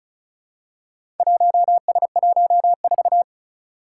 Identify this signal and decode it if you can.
Morse code, heard as a keyed tone, about 36 wpm, 700 Hz. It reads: 1S14
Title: Four Challenging Letter-Number Combinations 35wpm